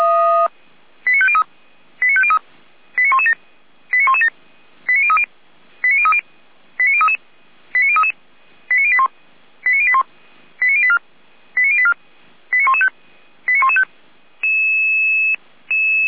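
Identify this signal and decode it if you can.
Five-tone selective call (ZVEI), heard as quick runs of five tones, tones of about 70 ms. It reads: 89503 89503 89117 89117 89030 89030 89033 89033 89081 89081 89084 89084 89115 89115
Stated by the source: ZVEI five-tone call 89080 received by SDR.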